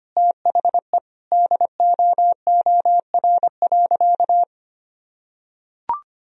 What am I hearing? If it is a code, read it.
Morse code, heard as a keyed tone, about 25 wpm, 700 Hz. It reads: THE DOOR.
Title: Sentences from Top 400 Words and Abbreviations 25wpm